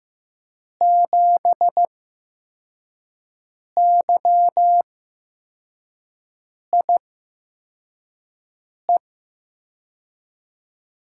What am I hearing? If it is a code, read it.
Morse code, heard as a keyed tone, about 15 wpm, spaced as for 4 wpm, 700 Hz. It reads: 7YIE
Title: USA Call Signs - 6 Characters - 8x Character Spacing - 15wpm